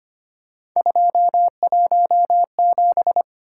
Morse code, heard as a keyed tone, about 25 wpm, 700 Hz. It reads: 217
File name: Three Letter-Number Combinations - 25wpm